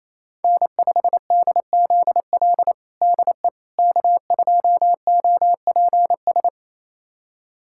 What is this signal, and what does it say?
Morse code, heard as a keyed tone, about 27 wpm, 700 Hz. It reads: N5DZL DE K2OPS
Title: USA Callsign DE Callsign - 5 Characters - 28wpm